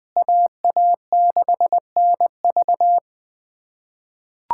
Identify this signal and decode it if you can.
Morse code, heard as a keyed tone, about 20 wpm, 700 Hz. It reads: AA6NV
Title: USA Callsign DE Callsign - 5 Characters - 20wpm